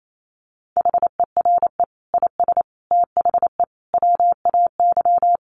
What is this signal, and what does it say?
Morse code, heard as a keyed tone, about 28 wpm, 700 Hz. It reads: HERE IS THE WAY
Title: Sentences from Top 200 Words 28wpm